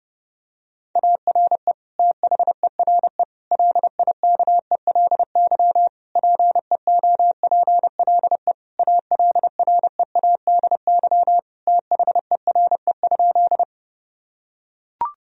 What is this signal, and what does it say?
Morse code, heard as a keyed tone, about 32 wpm, 700 Hz. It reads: ARE THERE LIKELY PEOPLE ALREADY THERE?
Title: Sentences from Top 1000 Words - Review Part 4 - 30wpm